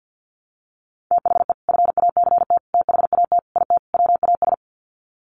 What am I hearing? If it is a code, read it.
Morse code, heard as a keyed tone, about 50 wpm, 700 Hz. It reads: THE FACT THAT IT WAS
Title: Sets of 5 Words Encore2 50wpm